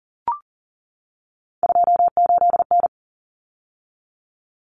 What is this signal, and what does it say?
Morse code, heard as a keyed tone, about 40 wpm, 700 Hz. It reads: 28N
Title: Common Cut Numbers - 3-Digit 40wpm